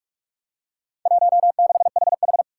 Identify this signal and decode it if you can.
Morse code, heard as a keyed tone, about 44 wpm, 700 Hz. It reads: 16HH
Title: Four Challenging Letter-Number Combinations 45wpm